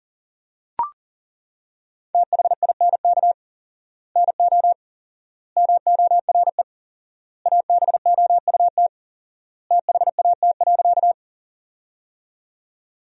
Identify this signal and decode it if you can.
Morse code, heard as a keyed tone, about 40 wpm, 700 Hz. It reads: THINK NO MORE ABOUT THAT.
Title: 4x Word-Spacing - Sentences from Top 200 Words - 40wpm